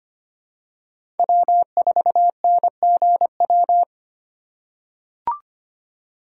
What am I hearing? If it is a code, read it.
Morse code, heard as a keyed tone, about 25 wpm, 700 Hz. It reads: W4NGW
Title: USA Call Signs - 5 Characters - 4x Character Spacing - 25wpm